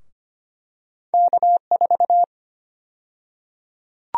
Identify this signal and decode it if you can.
Morse code, heard as a keyed tone, about 25 wpm, 700 Hz. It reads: K4
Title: USA Call Sign Prefixes - Repeated 2x - 25wpm